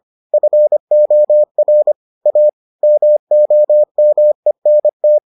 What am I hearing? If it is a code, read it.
Morse code, heard as a keyed tone, about 25 wpm, 600 Hz. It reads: FOR A MOMENT